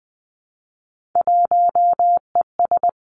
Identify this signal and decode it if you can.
Morse code, heard as a keyed tone, about 20 wpm, 700 Hz. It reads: 1ES
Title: Three Challenging Letter-Number Combinations 20wpm